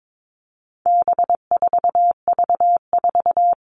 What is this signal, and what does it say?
Morse code, heard as a keyed tone, about 22 wpm, 700 Hz. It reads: B4V4
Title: All Cut Numbers - 4-Digit 22wpm